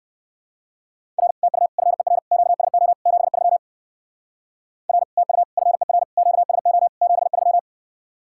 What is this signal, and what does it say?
Morse code, heard as a keyed tone, about 56 wpm, 700 Hz. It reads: R TU FER QSO 73 R TU FER QSO 73
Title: Speed-Racing Common QSO Phrases - 45wpm